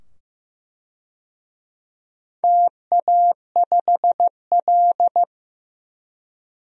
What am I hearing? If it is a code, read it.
Morse code, heard as a keyed tone, about 15 wpm, 700 Hz. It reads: TA5L
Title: DXCC-100 Call Signs - 4 Characters - 15wpm